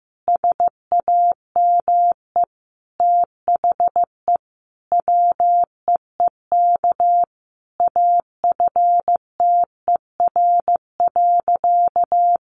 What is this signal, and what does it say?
Morse code, heard as a keyed tone, about 15 wpm, 700 Hz. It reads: SAME THE WEEK AFTER.
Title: Sentences from Top 500 Words 15wpm